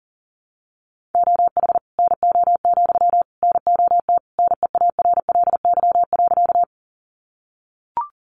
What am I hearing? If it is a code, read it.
Morse code, heard as a keyed tone, about 40 wpm, 700 Hz. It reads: OH NO, NOT NEARLY.
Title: Sentences from Top 1300 Words 40wpm